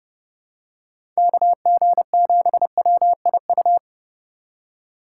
Morse code, heard as a keyed tone, about 32 wpm, 700 Hz. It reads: KG7WIU
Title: USA Call Signs - 6 Characters - 2x Character Spacing - 30wpm